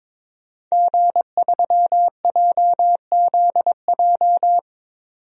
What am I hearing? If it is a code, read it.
Morse code, heard as a keyed tone, about 22 wpm, 700 Hz. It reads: G3JZJ
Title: DXCC-100 Call Signs - 5 Characters - 22wpm